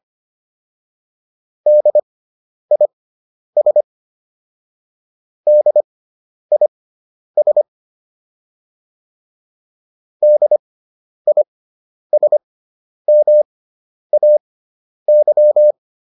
Morse code, heard as a keyed tone, about 25 wpm, spaced as for 10 wpm, 600 Hz. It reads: DIS DIS DISMAY